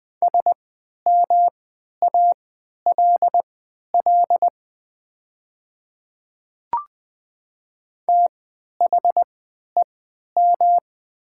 Morse code, heard as a keyed tone, about 20 wpm, spaced as for 11 wpm, 700 Hz. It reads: SMALL THEM
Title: 3x Character Spacing - Top 200 Words - 20wpm